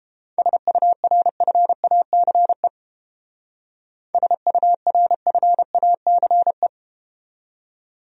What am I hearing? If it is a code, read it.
Morse code, heard as a keyed tone, about 30 wpm, 700 Hz. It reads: SURFACE SURFACE